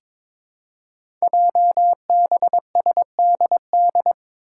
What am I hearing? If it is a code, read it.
Morse code, heard as a keyed tone, about 22 wpm, 700 Hz. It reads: JBSDD
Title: Five Challenging Letter-Number Combinations 22wpm